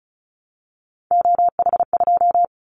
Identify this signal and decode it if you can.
Morse code, heard as a keyed tone, about 34 wpm, 700 Hz. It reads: OH2
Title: Callsign Prefixes for Top 50 DXCC - 35wpm